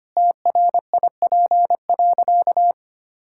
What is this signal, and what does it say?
Morse code, heard as a keyed tone, about 25 wpm, 700 Hz. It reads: TRIP.
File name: practice / 25wpm / Sentences with 2-Word Phrases - 25wpm